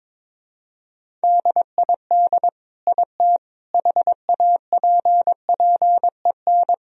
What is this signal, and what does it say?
Morse code, heard as a keyed tone, about 22 wpm, 700 Hz. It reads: DID IT HAPPEN
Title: Sentences from Top 1000 Words - Review Part 1 - 22wpm